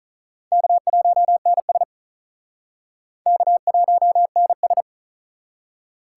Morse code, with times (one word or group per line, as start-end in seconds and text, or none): 0.5-1.8 K1NS
3.3-4.8 K1NS